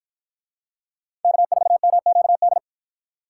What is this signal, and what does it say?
Morse code, heard as a keyed tone, about 53 wpm, 700 Hz. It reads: K4MQD